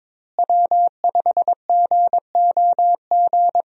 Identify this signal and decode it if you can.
Morse code, heard as a keyed tone, about 22 wpm, 700 Hz. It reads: W5GOG
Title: USA Call Signs - 5 Characters - Repeated 2x - 22wpm